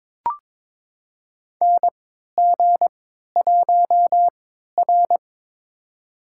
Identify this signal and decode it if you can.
Morse code, heard as a keyed tone, about 23 wpm, spaced as for 13 wpm, 700 Hz. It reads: NG1R
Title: USA Call Signs - 4 Characters - 3x Character Spacing - 22wpm